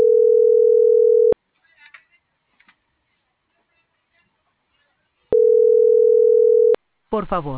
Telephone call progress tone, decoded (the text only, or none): ringback tone